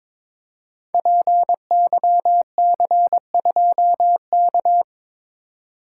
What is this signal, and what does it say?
Morse code, heard as a keyed tone, about 23 wpm, 700 Hz. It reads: PYC2K
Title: Code Groups - 5 Letter:Numbers - 22wpm